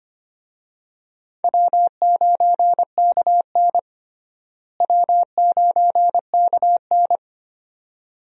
Morse code, heard as a keyed tone, about 25 wpm, 700 Hz. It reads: W9KN W9KN
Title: USA Call Signs - 4 Characters - Repeated 2x - 25wpm